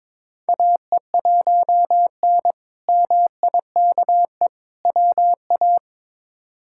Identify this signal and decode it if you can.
Morse code, heard as a keyed tone, about 21 wpm, 700 Hz. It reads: AE1N MIKE WA
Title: K1USN SST QSO Practice - 22wpm